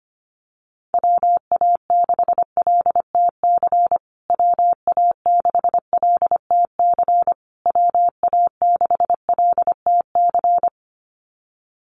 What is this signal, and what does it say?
Morse code, heard as a keyed tone, about 25 wpm, 700 Hz. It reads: WA6LTC WA6LTC WA6LTC
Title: USA Call Signs Repeated 3x 25wpm